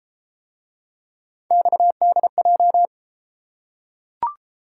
Morse code, heard as a keyed tone, about 29 wpm, 700 Hz. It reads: XDJ